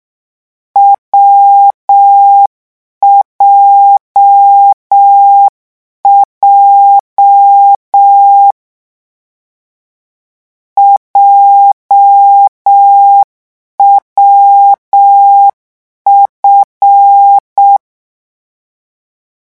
Morse code, heard as a keyed tone, about 6 wpm, 800 Hz. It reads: WJJ JWF